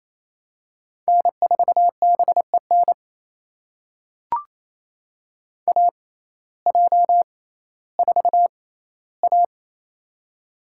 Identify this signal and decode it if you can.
Morse code, heard as a keyed tone, about 28 wpm, spaced as for 15 wpm, 700 Hz. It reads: N4BEN AJ4A
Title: USA Call Signs - 6x Character Spacing - 28wpm